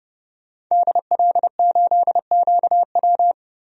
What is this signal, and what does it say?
Morse code, heard as a keyed tone, about 30 wpm, 700 Hz. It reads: DL8QW
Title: DXCC-100 Call Signs - 5 Characters - 30wpm